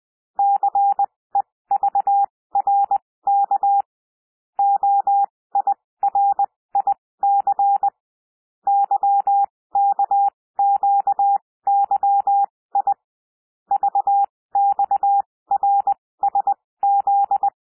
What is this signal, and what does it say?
Morse code, heard as a keyed tone, about 20 wpm, spaced as for 16 wpm, 800 Hz. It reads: CEVRK OIRIC YKQYI VXRSZ
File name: cwmp3 / L1 SOUND LETTERS 20 wpm